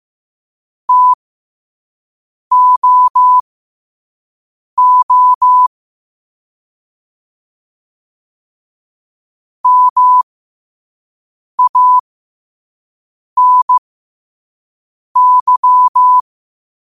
Morse code, heard as a keyed tone, about 15 wpm, spaced as for 5 wpm, 1000 Hz. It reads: TOO MANY
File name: 5wpm